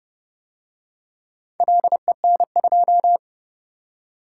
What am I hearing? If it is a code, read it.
Morse code, heard as a keyed tone, about 29 wpm, 700 Hz. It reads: LEN2